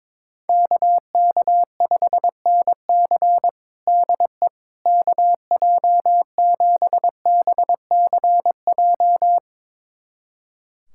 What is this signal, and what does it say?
Morse code, heard as a keyed tone, about 22 wpm, 700 Hz. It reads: KK5NC DE KJ7BCJ
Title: USA Callsign DE Callsign 22wpm